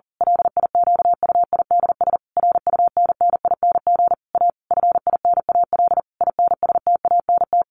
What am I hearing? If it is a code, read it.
Morse code, heard as a keyed tone, about 40 wpm, 700 Hz. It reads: LIQUIDS RUNNING A FINAL INSTANT
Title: A Sound Of Thunder 40wpm